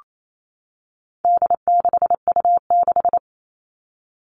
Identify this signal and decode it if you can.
Morse code, heard as a keyed tone, about 28 wpm, 700 Hz. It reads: D6U6
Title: All Cut Numbers - 4-Digit 28wpm